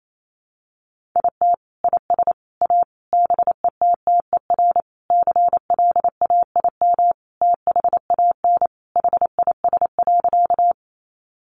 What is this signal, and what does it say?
Morse code, heard as a keyed tone, about 28 wpm, 700 Hz. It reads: IT IS A BETTER CLAIM THAN HIS.
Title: Sentences from Top 1500 Words 28wpm